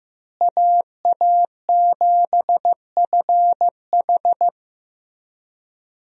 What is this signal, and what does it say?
Morse code, heard as a keyed tone, about 15 wpm, 700 Hz. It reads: AA7FH
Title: USA Call Signs - 5 Characters - Repeated 3x - 15wpm